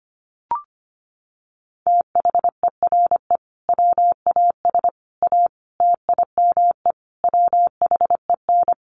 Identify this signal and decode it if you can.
Morse code, heard as a keyed tone, about 25 wpm, 700 Hz. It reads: THERE WAS A TIME WHEN
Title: Sets of 5 Words 25wpm